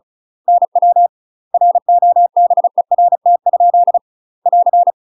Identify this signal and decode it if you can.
Morse code, heard as a keyed tone, about 34 wpm, 700 Hz. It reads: NW ROBERT? <AR>
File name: UK MASTER QSOs TRAINING / 35WPM / UK MASTER QSOs 0024